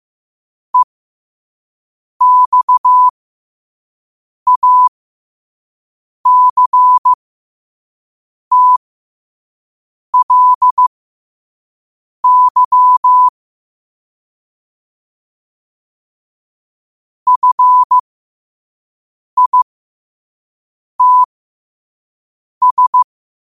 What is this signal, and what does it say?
Morse code, heard as a keyed tone, about 15 wpm, spaced as for 5 wpm, 1000 Hz. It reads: EXACTLY FITS
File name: qotdfiles / 5wpm